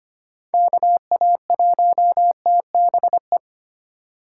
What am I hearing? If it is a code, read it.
Morse code, heard as a keyed tone, about 25 wpm, 700 Hz. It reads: KA1TBE
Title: USA Call Signs - 6 Characters - 3x Character Spacing - 25wpm